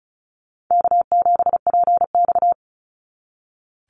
Morse code, heard as a keyed tone, about 34 wpm, 700 Hz. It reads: K7PX
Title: NA SOTA Activator Callsigns 35wpm